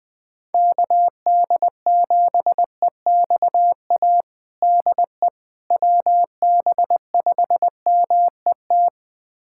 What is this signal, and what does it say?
Morse code, heard as a keyed tone, about 20 wpm, 700 Hz. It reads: KD7EXA DE WB5MET
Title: USA Callsign DE Callsign - 6 Characters - 20wpm